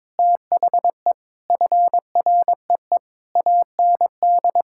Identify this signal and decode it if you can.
Morse code, heard as a keyed tone, about 22 wpm, 700 Hz. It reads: THE FREE AND
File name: Sets of 9 Words - 22wpm